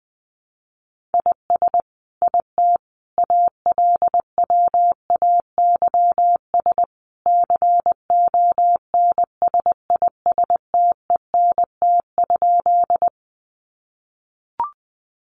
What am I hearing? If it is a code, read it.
Morse code, heard as a keyed tone, about 19 wpm, 700 Hz. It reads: IS IT ALWAYS CONSISTENT?
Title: Sentences from Top 1500 Words 20wpm